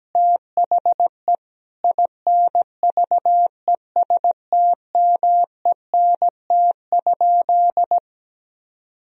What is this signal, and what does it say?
Morse code, heard as a keyed tone, about 17 wpm, 700 Hz. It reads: THE INVESTMENT?
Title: Sentences from Top 1000 Words - Review Part 4 - 17wpm